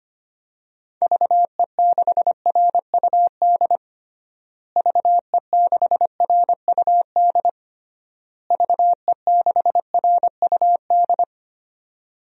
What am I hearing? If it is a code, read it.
Morse code, heard as a keyed tone, about 25 wpm, 700 Hz. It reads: VE6RUD VE6RUD VE6RUD
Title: DXCC-100 Call Signs - 6 Characters - Repeated 3x - 25wpm